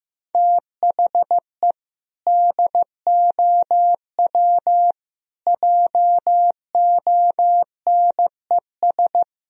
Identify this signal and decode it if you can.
Morse code, heard as a keyed tone, about 15 wpm, 700 Hz. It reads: THE DOW JONES
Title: Speed-Racing - Sets of 5 Words 15wpm